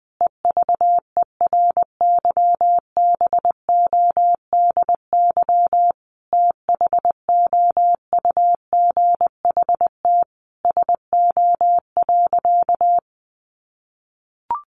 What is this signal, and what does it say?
Morse code, heard as a keyed tone, about 20 wpm, 700 Hz. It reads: EVERYBODY THOUGHT SO.
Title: Sentences from Top 1400 Words 20wpm